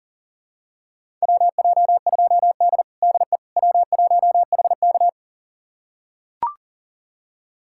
Morse code, heard as a keyed tone, about 40 wpm, 700 Hz. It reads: WJ2D DE W1HK
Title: USA Callsign DE Callsign - 4 Characters - 40wpm